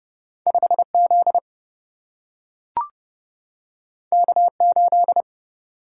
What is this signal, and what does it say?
Morse code, heard as a keyed tone, about 30 wpm, 700 Hz. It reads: 5 Z K 8
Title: Single-Letter Single-Number 30wpm